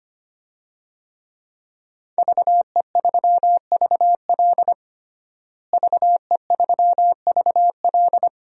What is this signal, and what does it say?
Morse code, heard as a keyed tone, about 25 wpm, 700 Hz. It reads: VE3VL VE3VL
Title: DXCC-100 Call Signs - 5 Characters - Repeated 2x - 25wpm